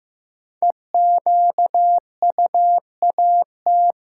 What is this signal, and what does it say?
Morse code, heard as a keyed tone, about 15 wpm, 700 Hz. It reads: EQUAT
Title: Sets of 5 Words - New Word Drill - 15wpm